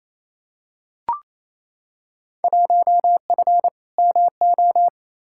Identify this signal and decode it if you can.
Morse code, heard as a keyed tone, about 28 wpm, 700 Hz. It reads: 1F MO
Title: Field Day Exchanges Encore - 28wpm